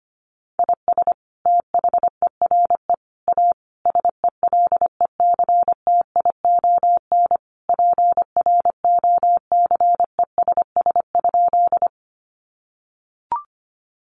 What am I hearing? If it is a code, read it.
Morse code, heard as a keyed tone, about 25 wpm, 700 Hz. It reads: IS THERE A SELECTION PROCESS?
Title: Sentences from Top 1400 Words 25wpm